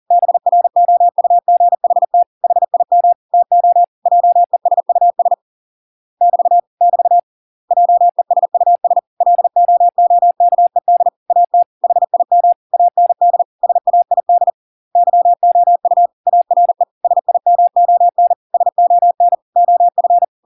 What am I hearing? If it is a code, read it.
Morse code, heard as a keyed tone, about 40 wpm, 700 Hz. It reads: BROUGHT HIM TO JESUS <BT> <BT> JESUS LOOKED AT HIM AND SAID YOU ARE SIMON SON OF